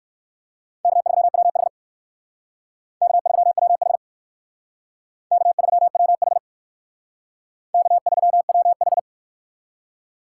Morse code, heard as a keyed tone, about 56 wpm, 700 Hz. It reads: K3WH K3WH K3WH K3WH